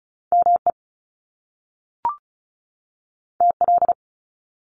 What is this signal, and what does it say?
Morse code, heard as a keyed tone, about 36 wpm, 700 Hz. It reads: ME TL
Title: Top Two Letter Combinations in Words 35wpm